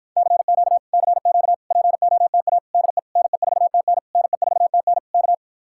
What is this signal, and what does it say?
Morse code, heard as a keyed tone, about 54 wpm, 700 Hz. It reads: CQ CQ POTA DE NE4TN NE4TN K